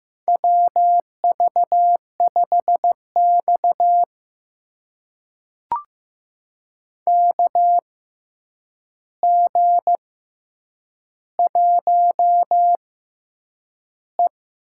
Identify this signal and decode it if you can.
Morse code, heard as a keyed tone, about 15 wpm, spaced as for 8 wpm, 700 Hz. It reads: WV5X KG1E
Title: USA Call Signs - 4 Characters - 6x Character Spacing - 15wpm